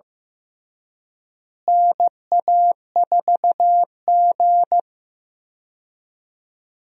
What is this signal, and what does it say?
Morse code, heard as a keyed tone, about 15 wpm, 700 Hz. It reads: NA4G
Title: USA Call Signs - 4 Characters - Repeated 3x - 15wpm